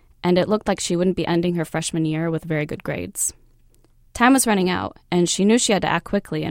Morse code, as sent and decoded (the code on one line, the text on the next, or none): none